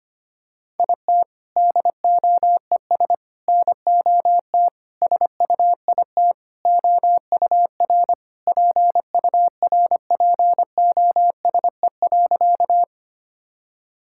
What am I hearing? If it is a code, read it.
Morse code, heard as a keyed tone, about 25 wpm, 700 Hz. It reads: IT DOES NOT SUIT OUR PURPOSE.
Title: Sentences from Top 1600 Words 25wpm